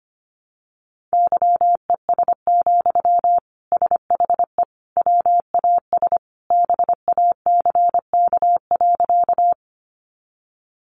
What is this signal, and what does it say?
Morse code, heard as a keyed tone, about 25 wpm, 700 Hz. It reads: YES, SHE WAS BACK.
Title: Sentences from Top 500 Words 25wpm